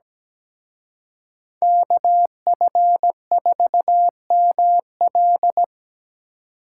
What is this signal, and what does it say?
Morse code, heard as a keyed tone, about 20 wpm, 700 Hz. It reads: KF4ML
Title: Speed Racing - USA 2x2 Call Signs 17wpm